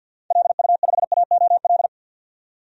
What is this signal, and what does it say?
Morse code, heard as a keyed tone, about 52 wpm, 700 Hz. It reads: LU5AOL
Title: DXCC-100 Call Signs - 6 Characters - 50wpm